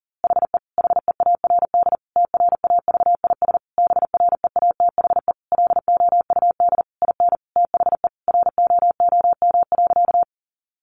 Morse code, heard as a keyed tone, about 40 wpm, 700 Hz. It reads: HE HEARD TRAVIS BREATHE LOUD IN THE ROOM.
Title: A Sound Of Thunder 40wpm